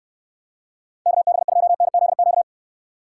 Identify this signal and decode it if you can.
Morse code, heard as a keyed tone, about 69 wpm, 700 Hz. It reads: KB2NZQ